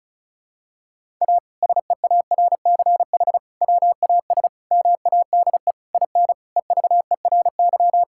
Morse code, heard as a keyed tone, about 36 wpm, 700 Hz. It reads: A SEARCH WAS MADE IN EVERY
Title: Sentences from Top 1000 Words - Review Part 5 - 35wpm